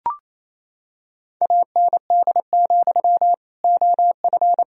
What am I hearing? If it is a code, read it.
Morse code, heard as a keyed tone, about 28 wpm, 700 Hz. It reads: AND, OF
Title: Sentences from Top 1000 Words - Review Part 1 - 28wpm